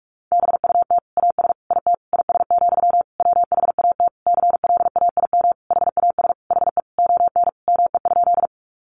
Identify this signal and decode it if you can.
Morse code, heard as a keyed tone, about 44 wpm, 700 Hz. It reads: BUT AS IT IS, WHAT CLAIM HAS HE ON ME?
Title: Sentences from Top 1500 Words 45wpm